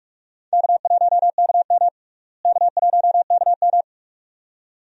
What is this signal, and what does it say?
Morse code, heard as a keyed tone, about 44 wpm, 700 Hz. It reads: K1KM K1KM
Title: USA Call Signs - 4 Characters - Repeated 2x - 45wpm